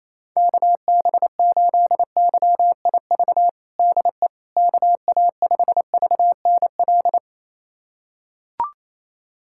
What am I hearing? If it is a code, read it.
Morse code, heard as a keyed tone, about 27 wpm, 700 Hz. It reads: KB8YIV DE KA5VNL
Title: USA Callsign DE Callsign - 6 Characters - 28wpm